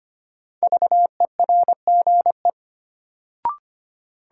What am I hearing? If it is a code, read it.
Morse code, heard as a keyed tone, about 25 wpm, 700 Hz. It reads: VERGE
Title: Sets of 4 Words - New Word Drill - 25wpm